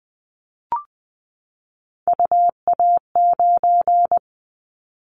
Morse code, heard as a keyed tone, about 20 wpm, 700 Hz. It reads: UA9
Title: Callsign Prefixes for Top 25 DXCC - 20wpm